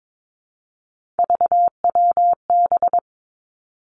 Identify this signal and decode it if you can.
Morse code, heard as a keyed tone, about 22 wpm, 700 Hz. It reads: VWB
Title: Three Challenging Letter-Number Combinations 22wpm